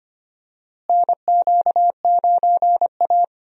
Morse code, heard as a keyed tone, about 25 wpm, 700 Hz. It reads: NQ9A
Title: USA Call Signs - 4 Characters - Repeated 2x - 25wpm